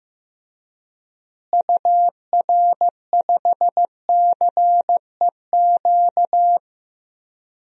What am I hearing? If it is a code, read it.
Morse code, heard as a keyed tone, about 15 wpm, 700 Hz. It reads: UR5CEQ
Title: DXCC-100 Call Signs - 6 Characters - Repeated 2x - 15wpm